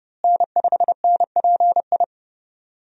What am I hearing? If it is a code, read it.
Morse code, heard as a keyed tone, about 31 wpm, 700 Hz. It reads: N5NPI
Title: USA Call Signs - 5 Characters - Repeated 2x - 30wpm